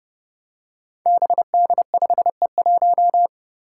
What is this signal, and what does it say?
Morse code, heard as a keyed tone, about 30 wpm, 700 Hz. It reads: BD5E1
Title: Five Challenging Letter-Number Combinations 30wpm